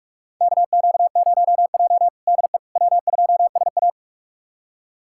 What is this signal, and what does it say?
Morse code, heard as a keyed tone, about 45 wpm, 700 Hz. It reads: KQ0J DE W2SA
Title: USA Callsign DE Callsign - 4 Characters - 45wpm